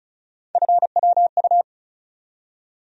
Morse code, heard as a keyed tone, about 36 wpm, 700 Hz. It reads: FWU
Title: USA Call Sign Postfixes - 3 Letters 35wpm